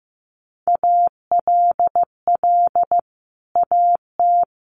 Morse code, heard as a keyed tone, about 14 wpm, 700 Hz. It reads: ALL AT
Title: Sentences from Top 1100 Words 15wpm